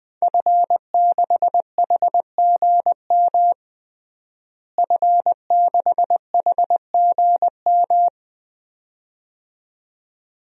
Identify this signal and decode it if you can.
Morse code, heard as a keyed tone, about 20 wpm, 700 Hz. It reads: F6HGM F6HGM
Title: DXCC-100 Call Signs - 5 Characters - Repeated 2x - 20wpm